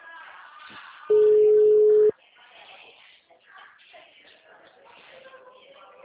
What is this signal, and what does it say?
Ringback tone (Europe, 425 Hz), with speech.